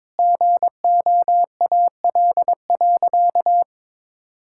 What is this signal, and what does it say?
Morse code, heard as a keyed tone, about 22 wpm, 700 Hz. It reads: GOAL.
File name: Sentences with 7-Word Phrases - 22wpm